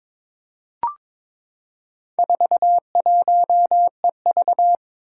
Morse code, heard as a keyed tone, about 22 wpm, 700 Hz. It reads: 41EV
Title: Four Challenging Letter-Number Combinations 22wpm